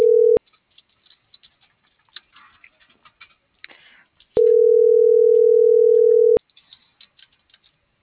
Ringback tone (North America), 440 + 480 Hz.